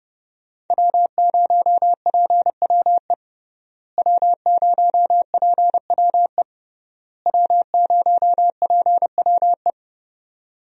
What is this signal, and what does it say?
Morse code, heard as a keyed tone, about 32 wpm, 700 Hz. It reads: W0PWE W0PWE W0PWE
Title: USA Call Signs - 5 Characters - Repeated 3x - 30wpm